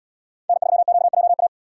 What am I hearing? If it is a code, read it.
Morse code, heard as a keyed tone, about 77 wpm, 700 Hz. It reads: N3QJA